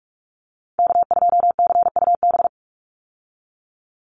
Morse code, heard as a keyed tone, about 44 wpm, 700 Hz. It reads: K2CUB
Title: Call Signs Repeated 3x - 45wpm